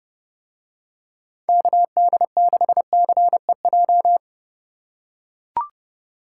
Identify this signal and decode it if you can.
Morse code, heard as a keyed tone, about 29 wpm, 700 Hz. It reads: KD6CEJ